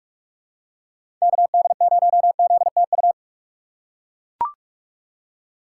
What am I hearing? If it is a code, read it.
Morse code, heard as a keyed tone, about 44 wpm, 700 Hz. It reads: KD0ZTU
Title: USA Call Signs - 6 Characters - 6x Character Spacing - 45wpm